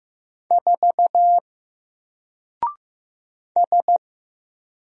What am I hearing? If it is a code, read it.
Morse code, heard as a keyed tone, about 15 wpm, 700 Hz. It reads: 4 S
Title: Single Character (Letter:Number:Punctuation:Prosign) Alt 15wpm